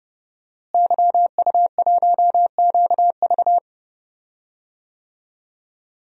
Morse code, heard as a keyed tone, about 30 wpm, 700 Hz. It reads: YU1QV
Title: DXCC-100 Call Signs - 5 Characters - Repeated 2x - 30wpm